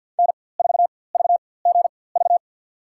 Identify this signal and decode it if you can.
Morse code, heard as a keyed tone, about 47 wpm, spaced as for 23 wpm, 700 Hz. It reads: N4VGV
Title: USA Call Signs - 5 Characters - 4x Character Spacing - 50wpm